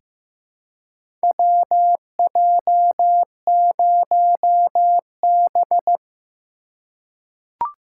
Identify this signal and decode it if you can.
Morse code, heard as a keyed tone, about 15 wpm, 700 Hz. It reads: WJ0B
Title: USA Call Signs - 4x Character Spacing - 15wpm